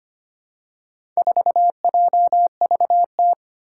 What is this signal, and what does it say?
Morse code, heard as a keyed tone, about 25 wpm, 700 Hz. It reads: 4JVT
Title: Four Challenging Letter-Number Combinations 25wpm